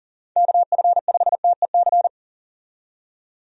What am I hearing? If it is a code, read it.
Morse code, heard as a keyed tone, about 40 wpm, 700 Hz. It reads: KF5TEC
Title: USA Call Signs - 6 Characters - 40wpm